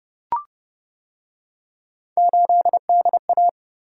Morse code, heard as a keyed tone, about 28 wpm, 700 Hz. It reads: 8DA